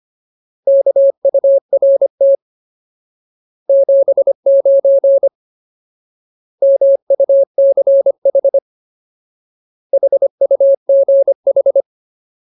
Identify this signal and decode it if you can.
Morse code, heard as a keyed tone, about 25 wpm, 550 Hz. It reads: KURT 79 MUCH HUGH
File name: Ultimate Word List / 25WPM Ultimate